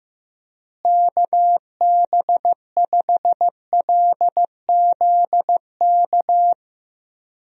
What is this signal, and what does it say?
Morse code, heard as a keyed tone, about 16 wpm, 700 Hz. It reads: KB5LZK
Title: USA Call Signs - 6 Characters - Repeated 2x - 15wpm